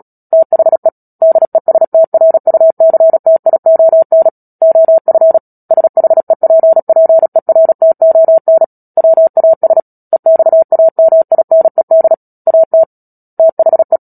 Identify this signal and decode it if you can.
Morse code, heard as a keyed tone, about 37 wpm, 650 Hz. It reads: THE DESTRUCTION OF SHEPPERTON WAS EXAMINED AT THE
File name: Book Two / 10-Chapter 10 - Epilogue - 36 WPM